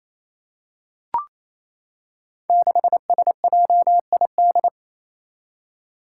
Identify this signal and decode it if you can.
Morse code, heard as a keyed tone, about 29 wpm, 700 Hz. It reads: 6SJID
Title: Five Challenging Letter-Number Combinations 28wpm